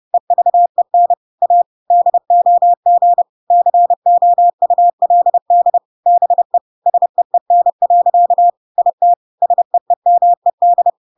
Morse code, heard as a keyed tone, about 30 wpm, 700 Hz. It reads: EVEN A DOG COULD BE SEEN. IT SEEMED
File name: Pinocchio / Pinocchio ch6 30 30 ik1hge